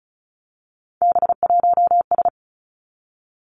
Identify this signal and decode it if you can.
Morse code, heard as a keyed tone, about 36 wpm, 700 Hz. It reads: B1S
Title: Three Challenging Letter-Number Combinations 35wpm